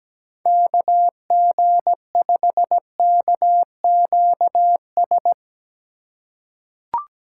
Morse code, heard as a keyed tone, about 17 wpm, 700 Hz. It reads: KG5KQS
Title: USA Call Signs - 2x Character Spacing - 17wpm